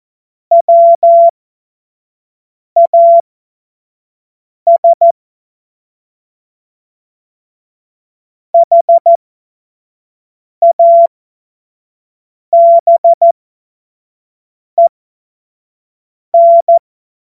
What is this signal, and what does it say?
Morse code, heard as a keyed tone, about 14 wpm, spaced as for 5 wpm, 680 Hz. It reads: WAS HABEN
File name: Morsekurs / Witze / Witz 20 Mai 2025 25 F70